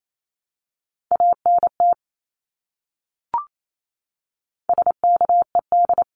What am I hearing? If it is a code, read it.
Morse code, heard as a keyed tone, about 28 wpm, 700 Hz. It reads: ANT SKED
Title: Abbreviations Repeated 3x - 28wpm